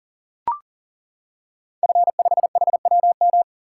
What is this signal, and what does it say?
Morse code, heard as a keyed tone, about 38 wpm, 700 Hz. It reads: F5HWM
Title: Code Groups - 5 Letter:Numbers - 40wpm